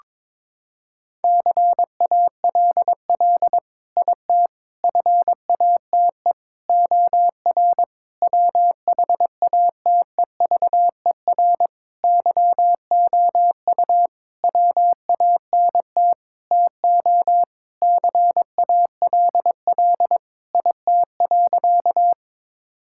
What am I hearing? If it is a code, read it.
Morse code, heard as a keyed tone, about 22 wpm, 700 Hz. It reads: CALL IT FATE OR WHATEVER YOU WANT TO CALL IT.
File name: Sentences with 7-Word Phrases - 22wpm